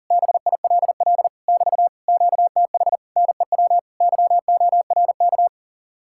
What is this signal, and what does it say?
Morse code, heard as a keyed tone, about 47 wpm, 700 Hz. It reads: BILL <BT> QTH NEW YORK
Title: Speed-Racing Common QSO Phrases Encore - 40wpm